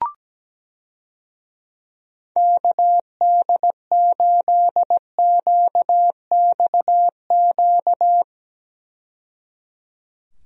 Morse code, heard as a keyed tone, about 17 wpm, 700 Hz. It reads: KD8QXQ
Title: USA Call Signs - 6 Characters - 17wpm